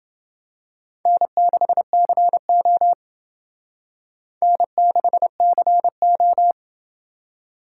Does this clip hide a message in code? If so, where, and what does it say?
Yes: Morse code from 1.1 to 6.5 s, heard as a keyed tone, about 28 wpm, 700 Hz. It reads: N6CO N6CO